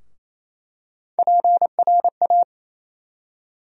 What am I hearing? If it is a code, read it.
Morse code, heard as a keyed tone, about 27 wpm, 700 Hz. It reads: PRA
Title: USA Call Sign Postfixes - 3 Letters 28wpm